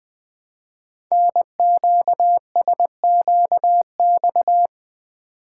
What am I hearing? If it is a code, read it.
Morse code, heard as a keyed tone, about 20 wpm, 700 Hz. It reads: NQSQX
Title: Code Groups - 5 Letters - 20wpm